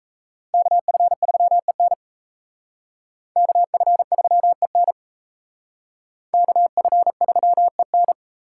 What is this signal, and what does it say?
Morse code, heard as a keyed tone, about 35 wpm, 700 Hz. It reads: KF3EN KF3EN KF3EN
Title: Speed Racing - USA 2x2 Call Signs 28wpm